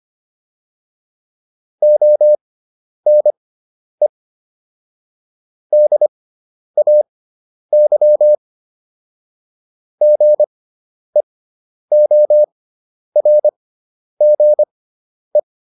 Morse code, heard as a keyed tone, about 25 wpm, spaced as for 10 wpm, 600 Hz. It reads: ONE DAY GEORGE